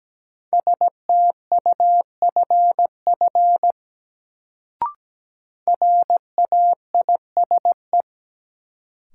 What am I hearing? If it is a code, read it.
Morse code, heard as a keyed tone, about 17 wpm, 700 Hz. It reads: STUFF RAISE
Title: Sentences from Top 1400 Words - New Word Drill - 17wpm